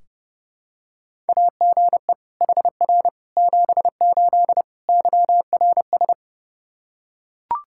Morse code, heard as a keyed tone, about 30 wpm, 700 Hz. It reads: AGE HR 78 YRS
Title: K1USN SST QSO Practice 30wpm